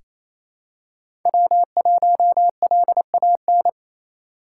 Morse code, heard as a keyed tone, about 28 wpm, 700 Hz. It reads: W1LAN
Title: USA Call Signs - 5 Characters - 2x Character Spacing - 28wpm